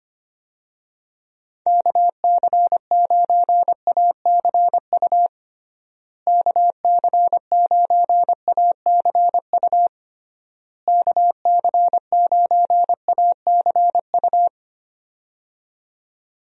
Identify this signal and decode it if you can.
Morse code, heard as a keyed tone, about 25 wpm, 700 Hz. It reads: KC9ACU KC9ACU KC9ACU
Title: USA Call Signs - 6 Characters - Repeated 3x - 25wpm